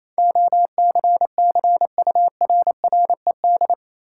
Morse code, heard as a keyed tone, about 28 wpm, 700 Hz. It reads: OCCURRED
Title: Sets of 5 Words Encore - New Word Drill - 28wpm